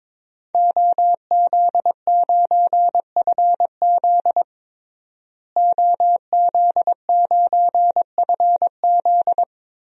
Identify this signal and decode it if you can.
Morse code, heard as a keyed tone, about 22 wpm, 700 Hz. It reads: OZ9FZ OZ9FZ
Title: DXCC-100 Call Signs - 5 Characters - Repeated 3x - 22wpm